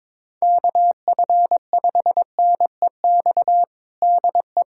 Morse code, heard as a keyed tone, about 22 wpm, 700 Hz. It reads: KF5NEX DE
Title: USA Callsign DE Callsign 22wpm